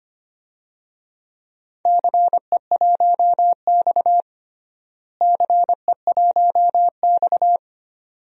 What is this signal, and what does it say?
Morse code, heard as a keyed tone, about 25 wpm, 700 Hz. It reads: CE1X CE1X
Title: DXCC-100 Call Signs - 4 Characters - Repeated 2x - 25wpm